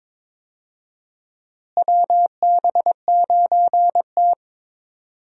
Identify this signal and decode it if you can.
Morse code, heard as a keyed tone, about 22 wpm, 700 Hz. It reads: WB9T
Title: USA Call Signs - 4 Characters - Repeated 2x - 22wpm